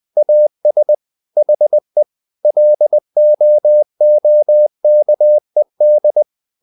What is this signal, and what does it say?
Morse code, heard as a keyed tone, about 20 wpm, 600 Hz. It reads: AS HE LOOKED
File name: the old man and the sea / the old man and the sea part 1 20wpm